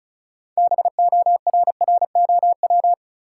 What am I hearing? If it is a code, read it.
Morse code, heard as a keyed tone, about 34 wpm, 700 Hz. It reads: BORROW
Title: Sentences from Top 1500 Words - New Word Drill Alt - 35wpm